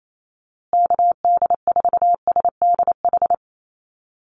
Morse code, heard as a keyed tone, about 28 wpm, 700 Hz. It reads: KD4SDH
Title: USA Call Signs Repeated 2x 28wpm